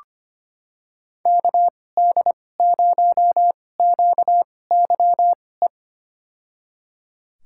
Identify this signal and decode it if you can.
Morse code, heard as a keyed tone, about 25 wpm, spaced as for 18 wpm, 700 Hz. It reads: KD0QYE
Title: USA Call Signs - 6 Characters - 2x Character Spacing - 25wpm